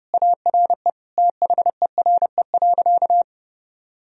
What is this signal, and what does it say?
Morse code, heard as a keyed tone, about 30 wpm, 700 Hz. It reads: ARE THERE.
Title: Sentences from Top 1000 Words - Review Part 1 - 30wpm